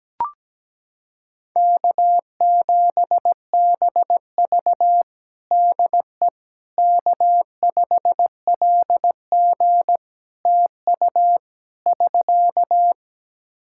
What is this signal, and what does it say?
Morse code, heard as a keyed tone, about 17 wpm, 700 Hz. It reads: K7BV DE K5LG TU <SK>
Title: K1USN SST QSO Practice 17wpm